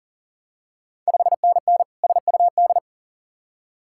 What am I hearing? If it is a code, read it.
Morse code, heard as a keyed tone, about 43 wpm, 690 Hz. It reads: 5NN SUD
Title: Ontario QSO Party Exchange 40wpm